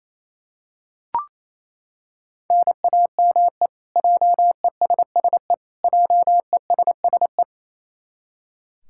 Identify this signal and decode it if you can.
Morse code, heard as a keyed tone, about 28 wpm, 700 Hz. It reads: NAME JESSE JESSE
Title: K1USN SST QSO Practice 28wpm